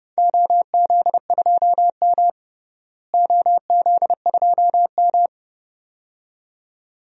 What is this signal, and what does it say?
Morse code, heard as a keyed tone, about 32 wpm, 700 Hz. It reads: OZ2M OZ2M
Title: DXCC-100 Call Signs - 4 Characters - Repeated 2x - 30wpm